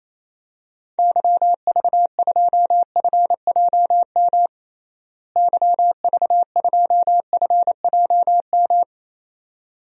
Morse code, heard as a keyed tone, about 28 wpm, 700 Hz. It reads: YV2FJM YV2FJM
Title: DXCC-100 Call Signs - 6 Characters - Repeated 2x - 28wpm